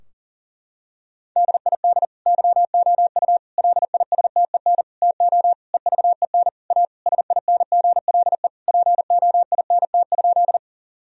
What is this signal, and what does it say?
Morse code, heard as a keyed tone, about 40 wpm, 700 Hz. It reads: DID YOU LISTEN TO EVEN A SINGLE POINT?
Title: Sentences from Top 1000 Words - Review Part 4 - 40wpm